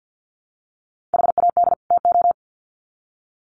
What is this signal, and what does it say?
Morse code, heard as a keyed tone, about 50 wpm, 700 Hz. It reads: HAD TO
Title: Sets of 2 Words 50wpm